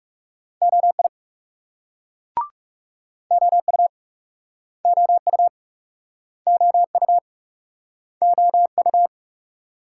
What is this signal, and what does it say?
Morse code, heard as a keyed tone, about 38 wpm, 700 Hz. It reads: OI OU OU OU OU